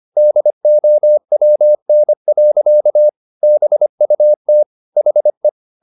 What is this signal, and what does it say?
Morse code, heard as a keyed tone, about 25 wpm, 600 Hz. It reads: DOWN. BUT HE
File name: the old man and the sea part 1 25wpm